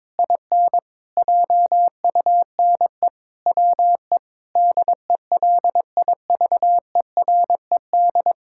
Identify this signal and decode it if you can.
Morse code, heard as a keyed tone, about 21 wpm, 700 Hz. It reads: IN JUNE WE DELIVERED
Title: Pangram Challenge v2 - 22wpm